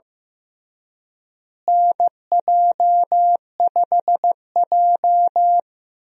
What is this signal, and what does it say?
Morse code, heard as a keyed tone, about 15 wpm, 700 Hz. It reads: NJ5J
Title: USA Call Signs - 4 Characters - Repeated 3x - 15wpm